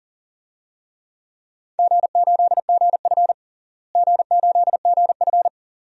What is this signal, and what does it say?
Morse code, heard as a keyed tone, about 40 wpm, 700 Hz. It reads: G8GF G8GF
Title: DXCC-100 Call Signs - 4 Characters - Repeated 2x - 40wpm